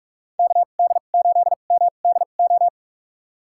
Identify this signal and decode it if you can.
Morse code, heard as a keyed tone, about 46 wpm, spaced as for 33 wpm, 700 Hz. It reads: KD8MDO
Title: USA Call Signs - 6 Characters - 2x Character Spacing - 45wpm